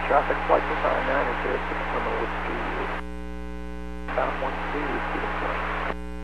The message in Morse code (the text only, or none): none